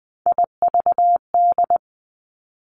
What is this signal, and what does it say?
Morse code, heard as a keyed tone, about 20 wpm, 700 Hz. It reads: IVD
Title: Three Challenging Letter-Number Combinations 20wpm